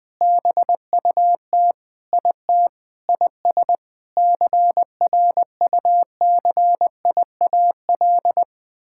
Morse code, heard as a keyed tone, about 20 wpm, 700 Hz. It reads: BUT IT IS CRUCIAL
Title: Sentences with 9-Word Phrases - 20wpm